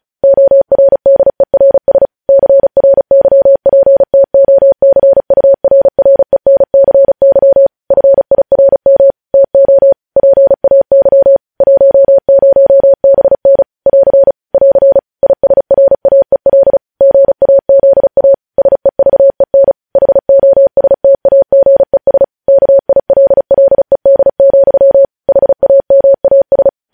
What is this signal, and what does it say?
Morse code, heard as a keyed tone, about 36 wpm, 550 Hz. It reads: ORDERS CRYPTOCURRENCY FIRM TO PAY 10BN <AR> <AR> ISRAEL GAZA SEVEN HOSTAGES KILLED, HAMAS
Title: News Headlines at 35